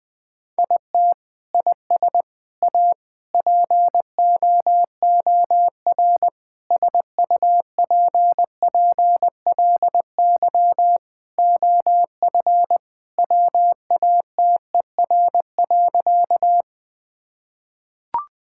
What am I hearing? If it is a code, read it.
Morse code, heard as a keyed tone, about 21 wpm, 700 Hz. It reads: IT IS A POOR SUPPLY OF WATER.
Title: Sentences from Top 1000 Words - Review Part 5 - 20wpm